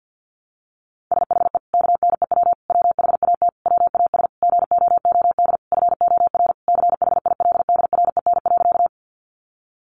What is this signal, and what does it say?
Morse code, heard as a keyed tone, about 50 wpm, 700 Hz. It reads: SHE KNEW WHAT WAS GOOD FOR CHILDREN.
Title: Sentences from Top 400 Words 50wpm